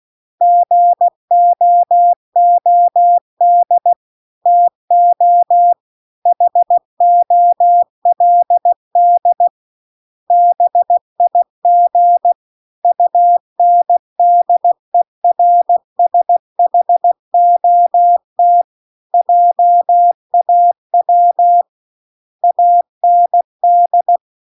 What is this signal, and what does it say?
Morse code, heard as a keyed tone, about 16 wpm, 700 Hz. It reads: GOOD TO HOLD BIG UNDERSHOT JAW AND